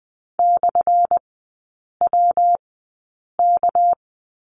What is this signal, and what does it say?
Morse code, heard as a keyed tone, about 20 wpm, spaced as for 8 wpm, 700 Hz. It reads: /WK
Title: 218 Lesson 35e - Number 0 - Three Characters 20wpm